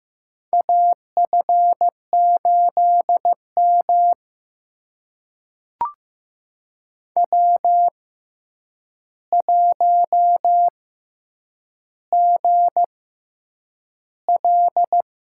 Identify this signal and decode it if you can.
Morse code, heard as a keyed tone, about 15 wpm, spaced as for 8 wpm, 700 Hz. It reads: AF8M W1GL
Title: USA Call Signs - 4 Characters - 6x Character Spacing - 15wpm